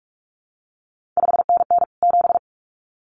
Morse code, heard as a keyed tone, about 46 wpm, 700 Hz. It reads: 5NN 7
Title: CQ WW DX Contest Exchange Encore - 45wpm